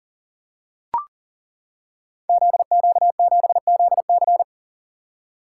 Morse code, heard as a keyed tone, about 41 wpm, 700 Hz. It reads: ZQ7ZC
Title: Code Groups - 5 Letter:Numbers - 40wpm